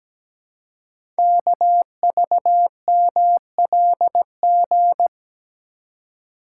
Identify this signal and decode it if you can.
Morse code, heard as a keyed tone, about 17 wpm, 700 Hz. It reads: KVMLG
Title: Code Groups - 5 Letters - 17wpm